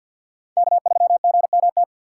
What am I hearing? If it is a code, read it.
Morse code, heard as a keyed tone, about 47 wpm, 700 Hz. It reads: K3GMT
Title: K1USN SST QSO Practice - 50wpm